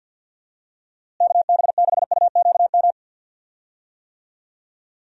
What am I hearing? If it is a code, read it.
Morse code, heard as a keyed tone, about 52 wpm, 700 Hz. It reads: KB6UQM
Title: USA Call Signs - 6 Characters - Repeated 2x - 50wpm